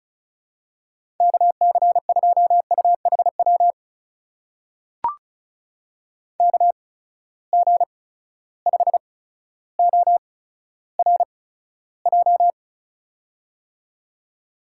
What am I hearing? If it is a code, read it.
Morse code, heard as a keyed tone, about 36 wpm, spaced as for 15 wpm, 700 Hz. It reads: KC2UHW KG5ORJ
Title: USA Call Signs - 6 Characters - 8x Character Spacing - 35wpm